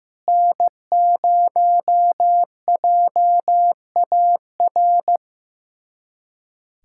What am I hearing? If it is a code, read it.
Morse code, heard as a keyed tone, about 15 wpm, 700 Hz. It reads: N0JAR
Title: USA Callsign DE Callsign - 5 Characters - 15wpm